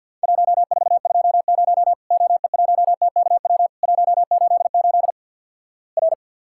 Morse code, heard as a keyed tone, about 50 wpm, 700 Hz. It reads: 1420 OE1TKW 188 R
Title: Worked All Europe DX Contest - QTC Exchange with Repeats - 50wpm